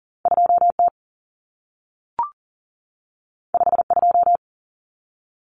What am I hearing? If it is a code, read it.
Morse code, heard as a keyed tone, about 40 wpm, 700 Hz. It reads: 2T 52
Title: Common Cut Numbers - 2-Digit 40wpm